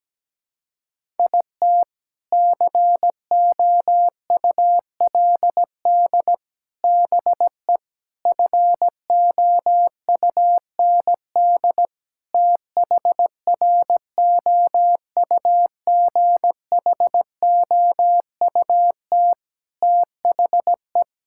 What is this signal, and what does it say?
Morse code, heard as a keyed tone, about 17 wpm, 700 Hz. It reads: IT COULD BE FOUND THROUGHOUT THE
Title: Sentences from Top 1000 Words - Review Part 5 - 17wpm